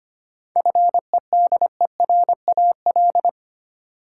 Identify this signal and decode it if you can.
Morse code, heard as a keyed tone, about 25 wpm, 700 Hz. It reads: FEDERAL
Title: Sentences from Top 1000 Words - Review Part 5 - New Word Drill - 25wpm